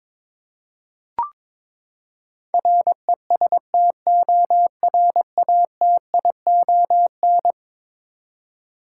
Morse code, heard as a keyed tone, about 22 wpm, 700 Hz. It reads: RESTORATION
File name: practice / 22wpm / Eleven Letter Words - 22wpm